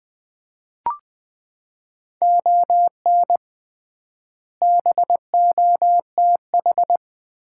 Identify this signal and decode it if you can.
Morse code, heard as a keyed tone, about 21 wpm, 700 Hz. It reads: ON BOTH
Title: 3x Word-Spacing - Sets of 4 Words - 20wpm